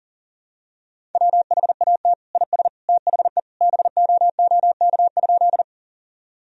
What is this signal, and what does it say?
Morse code, heard as a keyed tone, about 40 wpm, 700 Hz. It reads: WHAT IS THE BOOK?
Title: Sentences from Top 1000 Words - Review Part 1 - 40wpm